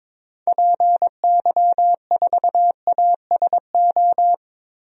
Morse code, heard as a keyed tone, about 22 wpm, 700 Hz. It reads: PY4ASO
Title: DXCC-100 Call Signs - 6 Characters - 22wpm